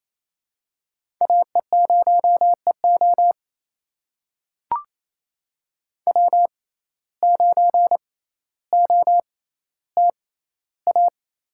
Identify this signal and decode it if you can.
Morse code, heard as a keyed tone, about 28 wpm, spaced as for 14 wpm, 700 Hz. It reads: AE0EO W9OTA
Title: USA Call Signs - 5 Characters - 6x Character Spacing - 28wpm